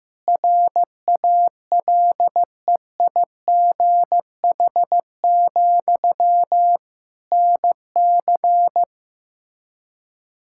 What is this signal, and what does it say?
Morse code, heard as a keyed tone, about 15 wpm, 700 Hz. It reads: RALEIGH, NC
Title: Practice QSO - Exchange 1 - Beginning - 15wpm